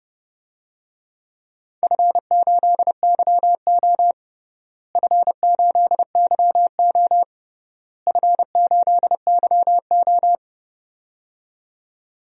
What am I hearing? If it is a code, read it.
Morse code, heard as a keyed tone, about 29 wpm, 700 Hz. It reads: F8YO F8YO F8YO
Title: DXCC-100 Call Signs - 4 Characters - Repeated 3x - 30wpm